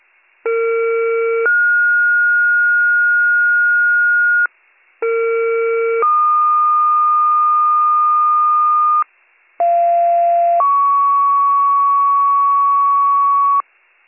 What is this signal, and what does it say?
Two-tone page, heard as a one-second tone, then a three-second tone, about 460 then 1430 Hz, about 460 then 1220 Hz, about 690 then 1120 Hz. It is heard from 0.5 to 13.6 s.